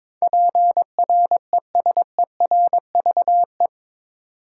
Morse code, heard as a keyed tone, about 22 wpm, 700 Hz. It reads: PRESERVE